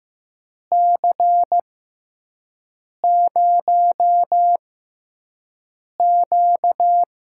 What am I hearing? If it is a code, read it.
Morse code, heard as a keyed tone, about 15 wpm, spaced as for 5 wpm, 700 Hz. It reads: C0Q
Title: USA Call Signs - 4 Characters - 6x Character Spacing - 15wpm